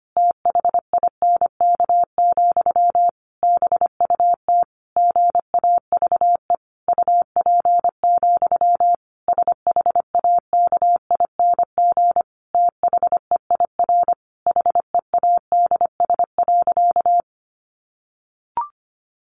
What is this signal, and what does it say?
Morse code, heard as a keyed tone, about 25 wpm, 700 Hz. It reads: THINK, BUT GAVE UP, SHAKING THEIR HEADS.
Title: A Sound Of Thunder 25wpm